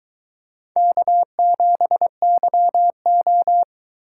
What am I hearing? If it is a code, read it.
Morse code, heard as a keyed tone, about 22 wpm, 700 Hz. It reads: K7YO